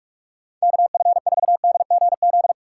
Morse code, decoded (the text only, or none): KF4DGZ